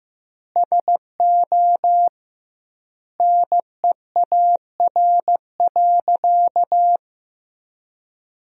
Morse code, heard as a keyed tone, about 15 wpm, 700 Hz. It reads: SO NEAR.
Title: 2x Word-Spacing - Sentences from Top 300 Words - 15wpm